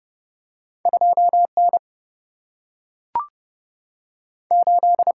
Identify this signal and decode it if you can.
Morse code, heard as a keyed tone, about 29 wpm, 700 Hz. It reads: 2 N 8
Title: Single-Letter Single-Number 30wpm